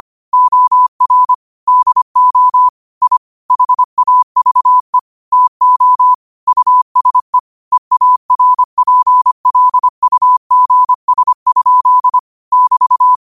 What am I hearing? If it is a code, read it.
Morse code, heard as a keyed tone, about 25 wpm, 1000 Hz. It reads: OR DO I HAVE TO USE EARPLUGS? <BT>
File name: qotdfiles / 25wpm